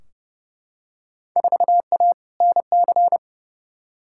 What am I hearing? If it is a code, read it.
Morse code, heard as a keyed tone, about 30 wpm, 700 Hz. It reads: 4A NC
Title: Field Day Exchanges - 30wpm